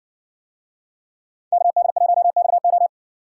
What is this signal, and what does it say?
Morse code, heard as a keyed tone, about 57 wpm, 700 Hz. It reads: KD1XO